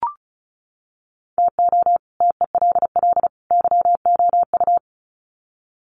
Morse code, heard as a keyed tone, about 34 wpm, 700 Hz. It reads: TO TELL YOU
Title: Sets of 3 Words 35wpm